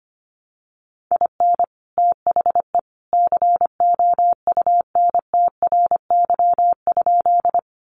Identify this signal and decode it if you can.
Morse code, heard as a keyed tone, about 25 wpm, 700 Hz. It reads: IN THE COUNTRY?
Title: Sentences from Top 300 Words 25wpm